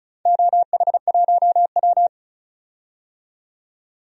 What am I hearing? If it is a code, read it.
Morse code, heard as a keyed tone, about 34 wpm, 700 Hz. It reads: OH1W
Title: DXCC-100 Call Signs - 4 Characters - 35wpm